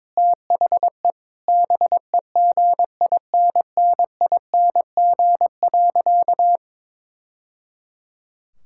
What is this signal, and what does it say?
Morse code, heard as a keyed tone, about 22 wpm, 700 Hz. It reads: THE BEGINNING.
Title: Sentences from Top 1000 Words - Review Part 5 - 22wpm